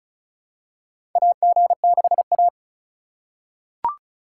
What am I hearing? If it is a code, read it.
Morse code, heard as a keyed tone, about 36 wpm, 700 Hz. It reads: AG6A
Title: USA Call Signs - 4 Characters - 3x Character Spacing - 35wpm